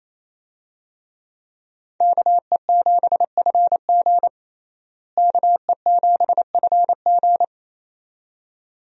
Morse code, heard as a keyed tone, about 28 wpm, 700 Hz. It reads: KE7FG KE7FG
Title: USA Call Signs - 5 Characters - Repeated 2x - 28wpm